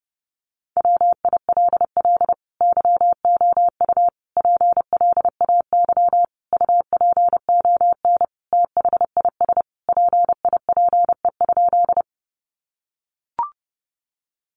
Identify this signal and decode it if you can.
Morse code, heard as a keyed tone, about 30 wpm, 700 Hz. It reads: WILL YOU PLAY UPON THIS PIPE?
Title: Sentences from Top 1500 Words 30wpm